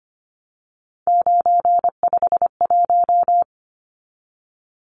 Morse code, heard as a keyed tone, about 25 wpm, 690 Hz. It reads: 951
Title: Numbers - 3-Digit 25wpm